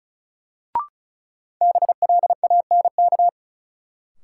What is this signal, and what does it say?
Morse code, heard as a keyed tone, about 36 wpm, 700 Hz. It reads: BLANK
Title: Sentences from Top 1500 Words - New Word Drill - 35wpm